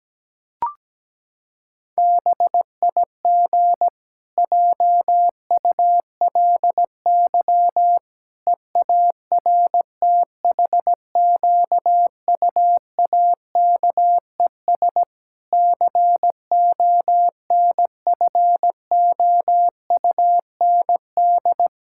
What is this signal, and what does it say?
Morse code, heard as a keyed tone, about 17 wpm, 700 Hz. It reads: BIG JULY EARTHQUAKES CONFOUND
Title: Pangram Challenge v2 - 17wpm